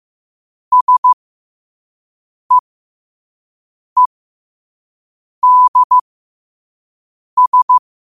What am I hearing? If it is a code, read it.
Morse code, heard as a keyed tone, about 15 wpm, spaced as for 5 wpm, 1000 Hz. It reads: SEEDS
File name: qotdfiles / 5wpm